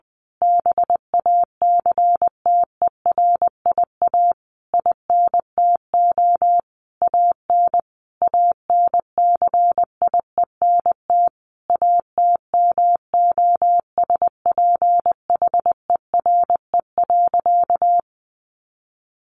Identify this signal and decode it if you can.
Morse code, heard as a keyed tone, about 20 wpm, 700 Hz. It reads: BACTERIA INTO AN ANCIENT ATMOSPHERE.
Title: A Sound Of Thunder 20wpm